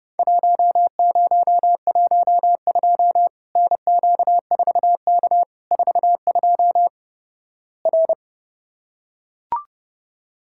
Morse code, heard as a keyed tone, about 30 wpm, 700 Hz. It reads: 1012 NQ4K 42 R
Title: Worked All Europe DX Contest - QTC Exchange - 30wpm